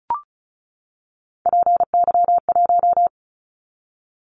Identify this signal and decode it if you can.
Morse code, heard as a keyed tone, about 36 wpm, 690 Hz. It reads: PY1
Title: Callsign Prefixes for Top 50 DXCC - 35wpm